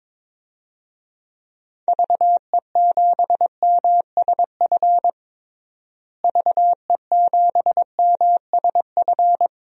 Morse code, heard as a keyed tone, about 22 wpm, 700 Hz. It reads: VE7MSF VE7MSF
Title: DXCC-100 Call Signs - 6 Characters - Repeated 2x - 22wpm